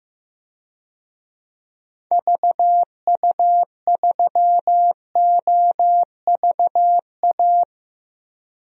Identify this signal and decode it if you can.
Morse code, heard as a keyed tone, about 15 wpm, 700 Hz. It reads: VU3OVA
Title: DXCC-100 Call Signs - 6 Characters - 15wpm